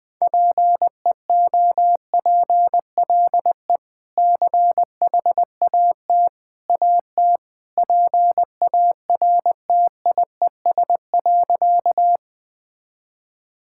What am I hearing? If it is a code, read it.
Morse code, heard as a keyed tone, about 20 wpm, 700 Hz. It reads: PEOPLE CHAT AT PARTIES.
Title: Easy Sentences with Hints - 20wpm